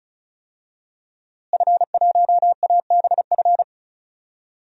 Morse code, heard as a keyed tone, about 36 wpm, 700 Hz. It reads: F1ABF
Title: DXCC-100 Call Signs - 5 Characters - Repeated 3x - 35wpm